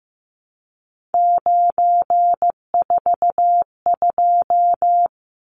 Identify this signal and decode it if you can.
Morse code, heard as a keyed tone, about 15 wpm, 700 Hz. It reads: 942
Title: Numbers - 3-Digit 15wpm